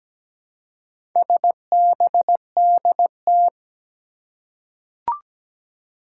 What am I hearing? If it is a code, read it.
Morse code, heard as a keyed tone, about 17 wpm, 700 Hz. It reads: SBDT
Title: Four Challenging Letter-Number Combinations 17wpm